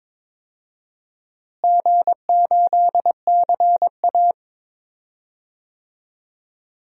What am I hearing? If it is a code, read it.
Morse code, heard as a keyed tone, about 21 wpm, 700 Hz. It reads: G8CA
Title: DXCC-100 Call Signs - 4 Characters - Repeated 2x - 22wpm